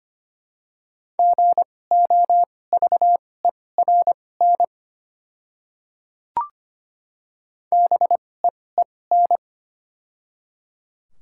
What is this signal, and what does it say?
Morse code, heard as a keyed tone, about 25 wpm, spaced as for 18 wpm, 700 Hz. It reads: GOVERN BEEN
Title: 2x Character Spacing - Top 500 Words - 25wpm